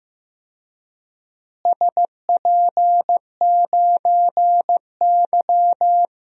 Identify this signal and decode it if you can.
Morse code, heard as a keyed tone, about 15 wpm, 700 Hz. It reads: SP9Y
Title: DXCC-100 Call Signs - 4 Characters - Repeated 2x - 15wpm